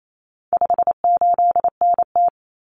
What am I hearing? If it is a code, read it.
Morse code, heard as a keyed tone, about 28 wpm, 700 Hz. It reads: 58NT
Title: Common Cut Numbers - 4-Digit 28wpm